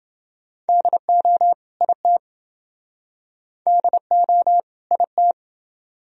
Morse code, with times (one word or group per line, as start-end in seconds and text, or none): 0.7-1.5 DO
1.8-2.2 IT
3.7-4.6 DO
4.9-5.3 IT